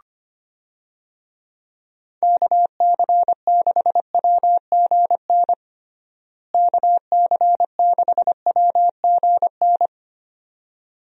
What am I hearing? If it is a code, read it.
Morse code, heard as a keyed tone, about 25 wpm, 700 Hz. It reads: KC6WGN KC6WGN
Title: USA Call Signs - 6 Characters - Repeated 2x - 25wpm